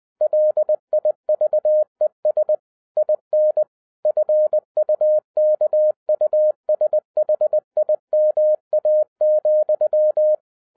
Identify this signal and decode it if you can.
Morse code, heard as a keyed tone, about 20 wpm, 610 Hz. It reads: LIVES IN FUKUSHIMA,